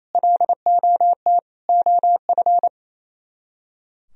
Morse code, heard as a keyed tone, about 35 wpm, 700 Hz. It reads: LOT OF